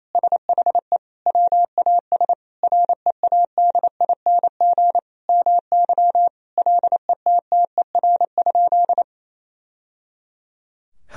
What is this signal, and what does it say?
Morse code, heard as a keyed tone, about 28 wpm, 700 Hz. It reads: SHE WAS READING MY LETTER?
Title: Sentences from Top 1000 Words - Review Part 3 - 28wpm